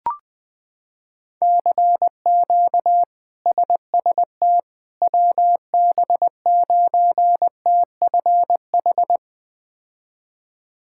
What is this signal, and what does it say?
Morse code, heard as a keyed tone, about 20 wpm, 700 Hz. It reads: CQ SST WB9TFH
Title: K1USN SST QSO Practice - 20wpm